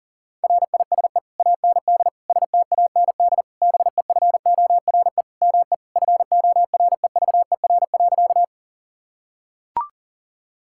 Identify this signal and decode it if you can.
Morse code, heard as a keyed tone, about 40 wpm, 700 Hz. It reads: RISE AND STAND BEFORE ME FOREVER.
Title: Sentences from Top 1600 Words 40wpm